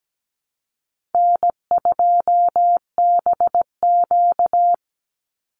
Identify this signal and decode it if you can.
Morse code, heard as a keyed tone, about 17 wpm, 700 Hz. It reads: N2BQ
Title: USA Call Signs 17wpm